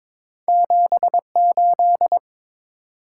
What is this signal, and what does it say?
Morse code, heard as a keyed tone, about 22 wpm, 700 Hz. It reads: 78
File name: practice / 22wpm / Two Letter-Number Combinations - 22wpm